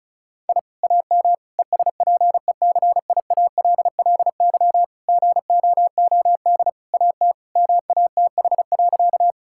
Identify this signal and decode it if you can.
Morse code, heard as a keyed tone, about 36 wpm, 700 Hz. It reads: I AM ESPECIALLY GOOD AT MATH.
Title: Sentences from Top 1000 Words - Review Part 6 - 35wpm